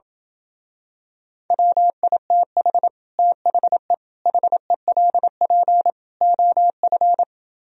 Morse code, heard as a keyed tone, about 28 wpm, 700 Hz. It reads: WITH THE HELP OF